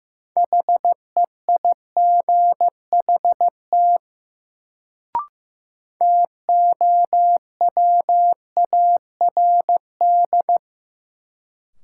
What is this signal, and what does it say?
Morse code, heard as a keyed tone, about 15 wpm, 700 Hz. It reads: HEIGHT TOWARD
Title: Sets of 5 Words - New Word Drill - 15wpm